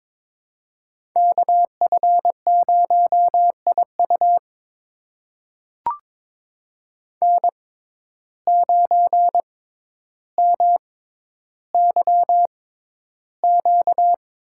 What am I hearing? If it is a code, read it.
Morse code, heard as a keyed tone, about 22 wpm, spaced as for 11 wpm, 700 Hz. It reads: KF0IU N9MYQ
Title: USA Call Signs - 5 Characters - 6x Character Spacing - 22wpm